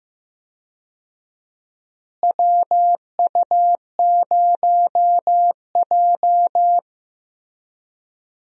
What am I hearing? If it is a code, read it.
Morse code, heard as a keyed tone, about 15 wpm, 700 Hz. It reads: WU0J
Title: USA Call Signs - 4 Characters - Repeated 3x - 15wpm